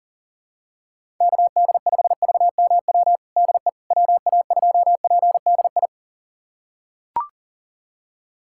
Morse code, heard as a keyed tone, about 40 wpm, 700 Hz. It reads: KD5VMW DE WA2PDI
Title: USA Callsign DE Callsign - 6 Characters - 40wpm